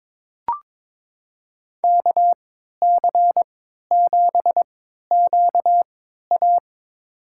Morse code, heard as a keyed tone, about 22 wpm, spaced as for 12 wpm, 700 Hz. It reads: KC7QA
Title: USA Call Signs - 5 Characters - 3x Character Spacing - 22wpm